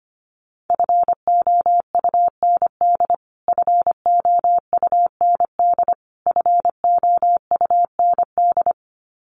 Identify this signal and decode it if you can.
Morse code, heard as a keyed tone, about 25 wpm, 700 Hz. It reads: FOUND FOUND FOUND
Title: Top 500 Words Repeated 25wpm